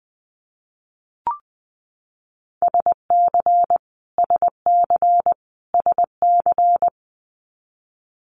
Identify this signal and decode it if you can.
Morse code, heard as a keyed tone, about 20 wpm, 700 Hz. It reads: SC SC SC
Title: Multi-letter Phonemes Repeated 3x 20wpm